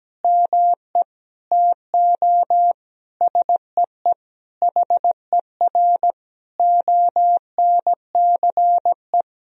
Morse code, heard as a keyed tone, about 17 wpm, 700 Hz. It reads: ME TO SEE HER ONCE
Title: Sentences from Top 500 Words and Abbreviations 17wpm